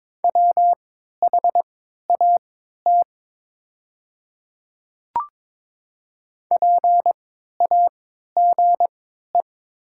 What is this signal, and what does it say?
Morse code, heard as a keyed tone, about 22 wpm, spaced as for 12 wpm, 700 Hz. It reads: WHAT PAGE
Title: 3x Character Spacing - Top 300 Words - 22wpm